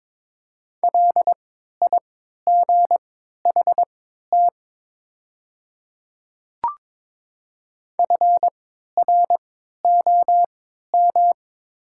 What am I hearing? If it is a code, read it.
Morse code, heard as a keyed tone, about 22 wpm, spaced as for 12 wpm, 700 Hz. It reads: LIGHT FROM
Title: 3x Character Spacing - Top 200 Words - 22wpm